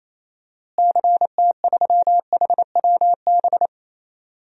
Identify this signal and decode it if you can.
Morse code, heard as a keyed tone, about 28 wpm, 700 Hz. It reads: CT3HWB
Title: DXCC-100 Call Signs - 6 Characters - Repeated 3x - 28wpm